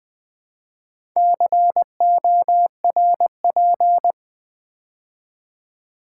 Morse code, heard as a keyed tone, about 20 wpm, 700 Hz. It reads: CORP
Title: Build A Word - 4 Letters At A Time - 11-Letter Words - 20wpm